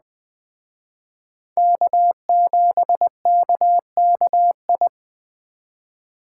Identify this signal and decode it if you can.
Morse code, heard as a keyed tone, about 20 wpm, 700 Hz. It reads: K7KKI
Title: Speed Racing - USA 1x3 Call Signs 17wpm